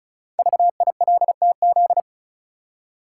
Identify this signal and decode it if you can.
Morse code, heard as a keyed tone, about 36 wpm, 690 Hz. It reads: VILTZ
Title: Code Groups - 5 Letters - 35wpm